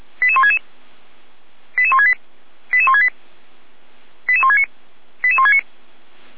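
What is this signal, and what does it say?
Five-tone selective call (ZVEI), heard as quick runs of five tones, tones of about 70 ms. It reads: 88166 88168 88168 88169 88169
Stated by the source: SDR-received ZVEI five-tone selective call.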